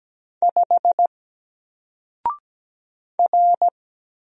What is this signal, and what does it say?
Morse code, heard as a keyed tone, about 17 wpm, 700 Hz. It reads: 5 R
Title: Single Letter-Number 17wpm